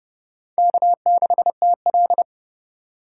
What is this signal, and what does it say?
Morse code, heard as a keyed tone, about 28 wpm, 700 Hz. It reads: K6TL